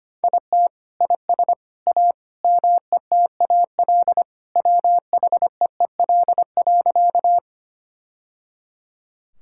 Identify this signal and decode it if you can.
Morse code, heard as a keyed tone, about 25 wpm, 700 Hz. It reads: IT IS A METAL WHEEL.
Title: Sentences from Top 1000 Words - Review Part 6 - 25wpm